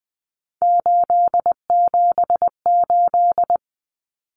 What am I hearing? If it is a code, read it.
Morse code, heard as a keyed tone, about 20 wpm, 700 Hz. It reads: 878
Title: Common Cut Numbers - 3-Digit 20wpm